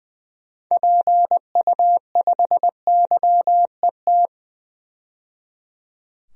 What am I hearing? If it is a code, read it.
Morse code, heard as a keyed tone, about 21 wpm, 700 Hz. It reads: PU5YET
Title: DXCC-100 Call Signs - 6 Characters - 20wpm